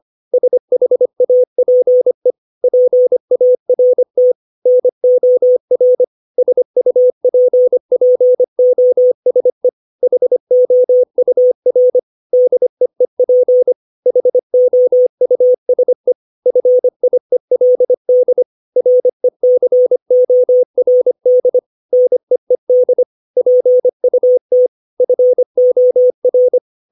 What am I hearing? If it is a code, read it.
Morse code, heard as a keyed tone, about 25 wpm, 500 Hz. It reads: SHAPE PART NOR SUPPOSE HOUR DEEP HOUSE FIELD RECORD NEED PUT FOR
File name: 632 words / 632 Words 25WPM